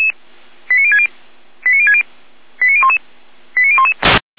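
Five-tone selective call (ZVEI), heard as quick runs of five tones, tones of about 70 ms. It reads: 89060 89060 89011 89011
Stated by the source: ZVEI five-tone call 89130 received by SDR, recorded by feuerwehr-orlamuende.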